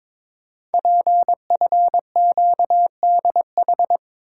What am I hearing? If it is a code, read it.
Morse code, heard as a keyed tone, about 22 wpm, 700 Hz. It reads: PFQDH
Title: Code Groups - 5 Letters - 22wpm